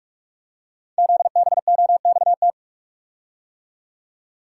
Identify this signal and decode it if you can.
Morse code, heard as a keyed tone, about 44 wpm, 690 Hz. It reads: ZBOXT
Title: Code Groups - 5 Letters - 45wpm